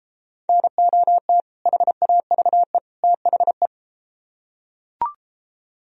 Morse code, heard as a keyed tone, about 32 wpm, 700 Hz. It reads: NOT HAVE THE